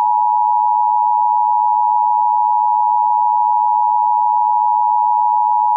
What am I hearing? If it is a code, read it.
Emergency Alert System (SAME) tones (the 853 and 960 Hz attention tone). It is heard from 0.9 to 5.5 s.